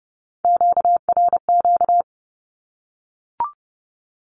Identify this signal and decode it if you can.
Morse code, heard as a keyed tone, about 30 wpm, 700 Hz. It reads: QRQ
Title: Q-Codes Repeated 3x 30wpm